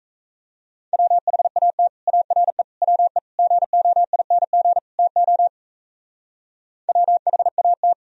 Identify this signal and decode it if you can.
Morse code, heard as a keyed tone, about 34 wpm, 700 Hz. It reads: WHAT ARE WE GOING TO WHAT
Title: Speed-Racing - Sets of 5 Words 28wpm